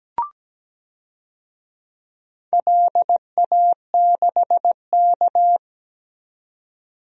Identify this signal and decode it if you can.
Morse code, heard as a keyed tone, about 17 wpm, 700 Hz. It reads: LA6K
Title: DXCC-100 Call Signs - 4 Characters - 17wpm